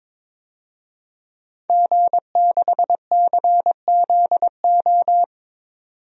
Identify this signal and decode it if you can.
Morse code, heard as a keyed tone, about 22 wpm, 700 Hz. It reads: G6CZO
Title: DXCC-100 Call Signs - 5 Characters - 22wpm